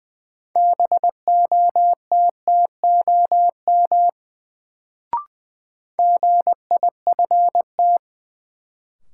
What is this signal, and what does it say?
Morse code, heard as a keyed tone, about 20 wpm, 700 Hz. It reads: BOTTOM GIFT
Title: Sentences from Top 900 Words - New Word Drill - 20wpm